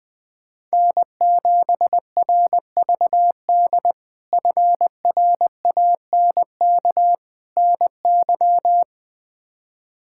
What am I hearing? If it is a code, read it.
Morse code, heard as a keyed tone, about 20 wpm, 700 Hz. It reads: N7RVD FRANK NY
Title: K1USN SST QSO Practice - 20wpm